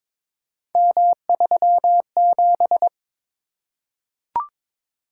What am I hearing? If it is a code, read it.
Morse code, heard as a keyed tone, about 21 wpm, 700 Hz. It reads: M37
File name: practice / 22wpm / Three Letter-Number Combinations - 22wpm